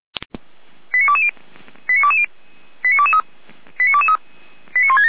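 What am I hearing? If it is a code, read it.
Five-tone selective call (ZVEI), heard as quick runs of five tones, tones of about 70 ms. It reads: 89220 89220 89223 89223 89167
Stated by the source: ZVEI five-tone selective call received with an SDR, logged on the Feuerwehr Orlamünde audio server.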